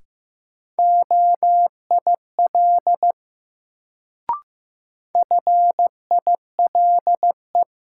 Morse code, heard as a keyed tone, about 15 wpm, 700 Hz. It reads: OIL FILE
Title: Sentences from Top 1000 Words - Review Part 4 - New Word Drill - 15wpm